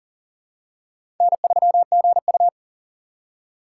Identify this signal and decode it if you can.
Morse code, heard as a keyed tone, about 40 wpm, 700 Hz. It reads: N3GU
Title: USA Call Signs - 4 Characters - 3x Character Spacing - 40wpm